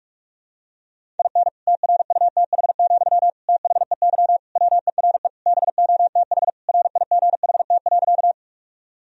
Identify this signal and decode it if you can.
Morse code, heard as a keyed tone, about 46 wpm, 700 Hz. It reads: IN TRUTH, THEY WERE BOTH RIGHT.
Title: Sentences from Top 1000 Words - Review Part 1 - 45wpm